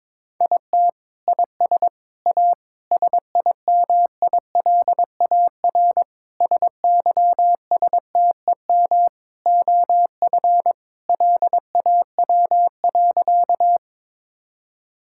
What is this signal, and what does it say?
Morse code, heard as a keyed tone, about 22 wpm, 700 Hz. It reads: IT IS A SIMILAR SYSTEM OF LAW.
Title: Sentences from Top 1000 Words - Review Part 5 - 22wpm